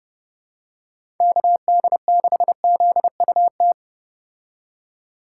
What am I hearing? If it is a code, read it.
Morse code, heard as a keyed tone, about 30 wpm, 700 Hz. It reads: KD6ZUT